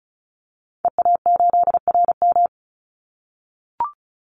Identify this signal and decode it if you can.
Morse code, heard as a keyed tone, about 34 wpm, 700 Hz. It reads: EA8RM
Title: Call Signs - 35wpm